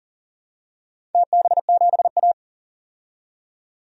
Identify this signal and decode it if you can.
Morse code, heard as a keyed tone, about 42 wpm, 700 Hz. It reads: TB7A
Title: DXCC-100 Call Signs - 4 Characters - 40wpm